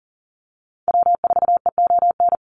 Morse code, heard as a keyed tone, about 40 wpm, 700 Hz. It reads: W4EON
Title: NA SOTA Activator Callsigns 40wpm